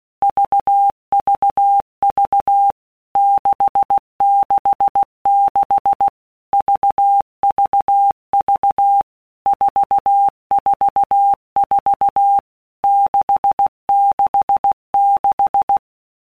Morse code, heard as a keyed tone, about 16 wpm, 800 Hz. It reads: VVV 666 VVV 444 666
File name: Basic Letter Groups By WPM's / Groups@15 / VB46@15